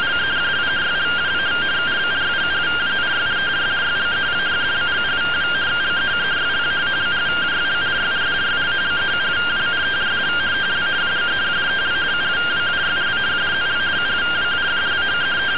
Two-tone teleprinter signal, about 170 Hz shift.
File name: findings / CT AXIAL -256.gen-08 rtty rx